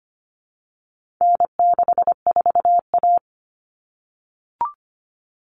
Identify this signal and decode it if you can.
Morse code, heard as a keyed tone, about 25 wpm, 700 Hz. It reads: N64A
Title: Common Cut Numbers - 4-Digit 25wpm